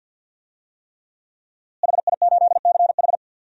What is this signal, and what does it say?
Morse code, heard as a keyed tone, about 50 wpm, 700 Hz. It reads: HI8CH
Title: DXCC-100 Call Signs - 5 Characters - Repeated 3x - 50wpm